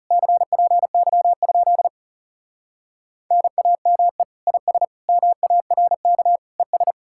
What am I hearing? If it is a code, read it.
Morse code, heard as a keyed tone, about 40 wpm, 700 Hz. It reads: CPY? NAME IS MARK ES